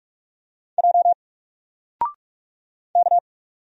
Morse code, heard as a keyed tone, about 45 wpm, 700 Hz. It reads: J K
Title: Single Letter-Number 45wpm